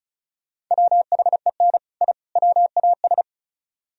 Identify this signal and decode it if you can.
Morse code, heard as a keyed tone, about 45 wpm, 700 Hz. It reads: WHEN I WAS